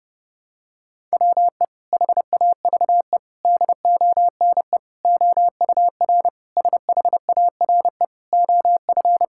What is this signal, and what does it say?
Morse code, heard as a keyed tone, about 30 wpm, 700 Hz. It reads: WE HAVE DONE OUR SHARE OF
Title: Sentences from Top 1000 Words - Review Part 6 - 30wpm